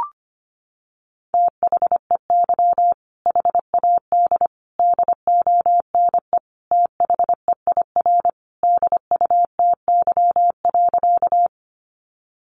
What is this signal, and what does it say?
Morse code, heard as a keyed tone, about 25 wpm, 700 Hz. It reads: THEY HAD DONE THEIR DUTY.
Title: Sentences from Top 1500 Words 25wpm